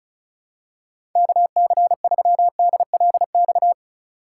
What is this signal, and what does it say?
Morse code, heard as a keyed tone, about 36 wpm, 700 Hz. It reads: KC3DLX
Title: USA Call Signs - 6 Characters - 4x Character Spacing - 35wpm